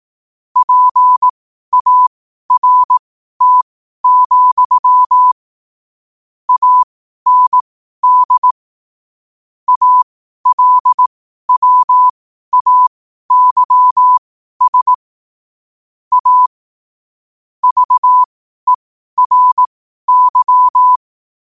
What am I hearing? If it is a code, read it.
Morse code, heard as a keyed tone, about 18 wpm, spaced as for 12 wpm, 1000 Hz. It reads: PART, AND ALWAYS A VERY
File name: 12wpm